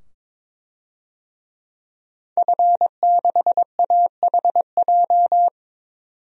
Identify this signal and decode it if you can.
Morse code, heard as a keyed tone, about 22 wpm, 700 Hz. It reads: F6AHJ
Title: DXCC-100 Call Signs - 5 Characters - 22wpm